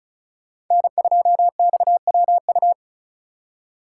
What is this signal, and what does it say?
Morse code, heard as a keyed tone, about 45 wpm, 700 Hz. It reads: N2XWU